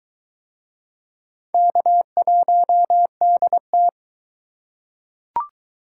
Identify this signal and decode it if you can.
Morse code, heard as a keyed tone, about 20 wpm, 700 Hz. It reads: K1DT